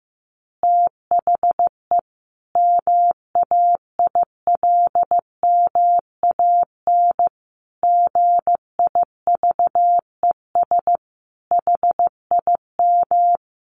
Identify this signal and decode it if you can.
Morse code, heard as a keyed tone, about 15 wpm, 700 Hz. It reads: THE MAILMAN GIVES HIM
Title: Additional Easy Phrases 15wpm